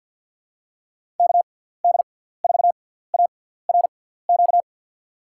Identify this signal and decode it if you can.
Morse code, heard as a keyed tone, about 53 wpm, spaced as for 17 wpm, 700 Hz. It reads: KD4ARQ
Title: USA Call Signs - 6 Characters - 6x Character Spacing - 50wpm